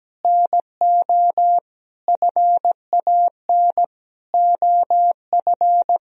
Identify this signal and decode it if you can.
Morse code, heard as a keyed tone, about 17 wpm, 700 Hz. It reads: NO FAN OF
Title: Sentences from Top 1000 Words - Review Part 4 - 17wpm